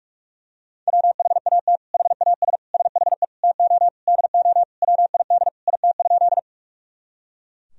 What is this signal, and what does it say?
Morse code, heard as a keyed tone, about 41 wpm, 700 Hz. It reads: WHAT HAS SHE TO DO WID IT?
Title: Sentences from Top 100 Words and Abbreviations 45wpm